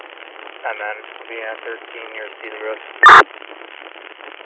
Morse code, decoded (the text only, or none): none